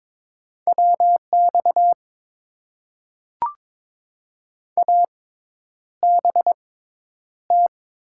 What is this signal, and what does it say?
Morse code, heard as a keyed tone, about 22 wpm, spaced as for 10 wpm, 700 Hz. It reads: WX ABT
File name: Top 90 QSO Elements - 6x Character Spacing - 22wpm